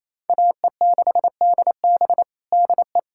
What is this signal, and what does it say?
Morse code, heard as a keyed tone, about 28 wpm, 700 Hz. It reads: AE6DB DE
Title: USA Callsign DE Callsign - 5 Characters - 28wpm